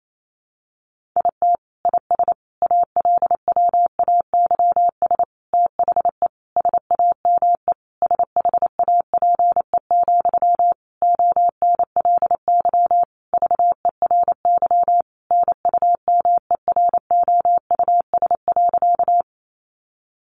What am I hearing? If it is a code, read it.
Morse code, heard as a keyed tone, about 28 wpm, 700 Hz. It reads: IT IS ALWAYS THE SAME SHAPE, ONLY VERY NUMEROUS.
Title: Sentences from Top 1400 Words 28wpm